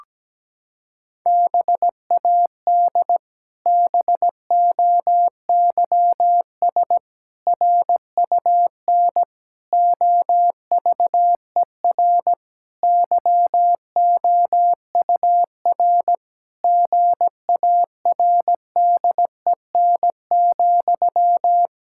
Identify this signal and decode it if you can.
Morse code, heard as a keyed tone, about 17 wpm, 700 Hz. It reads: BAD BOYS RUN OVER YOUR GARDEN,
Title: Pangram Challenge v2 - 17wpm